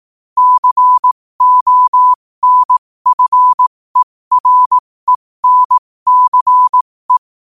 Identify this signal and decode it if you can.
Morse code, heard as a keyed tone, about 18 wpm, spaced as for 15 wpm, 1000 Hz. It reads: CONFERENCE
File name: qotdfiles / 15wpm